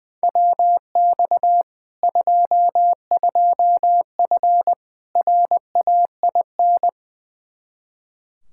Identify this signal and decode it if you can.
Morse code, heard as a keyed tone, about 20 wpm, 700 Hz. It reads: WX 22F RAIN
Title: Practice QSO - Exchange 2 - Info - 20wpm